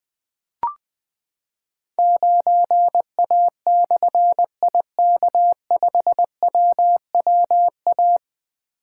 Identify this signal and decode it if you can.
Morse code, heard as a keyed tone, about 20 wpm, 700 Hz. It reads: 9A/IK5WWA
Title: Difficult Call Signs 20wpm